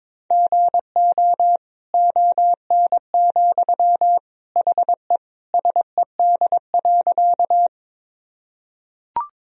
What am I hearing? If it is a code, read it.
Morse code, heard as a keyed tone, about 22 wpm, 700 Hz. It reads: GO ON, HE SED.
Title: Sentences from Top 100 Words and Abbreviations 22wpm